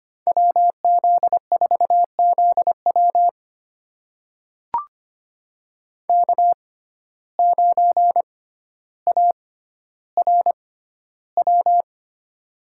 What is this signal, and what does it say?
Morse code, heard as a keyed tone, about 25 wpm, spaced as for 13 wpm, 700 Hz. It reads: WZ4ZW K9ARW
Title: USA Call Signs - 5 Characters - 6x Character Spacing - 25wpm